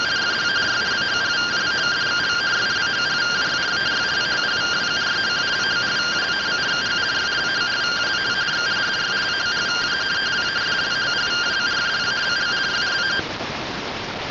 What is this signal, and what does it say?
Two-tone teleprinter signal, about 170 Hz shift.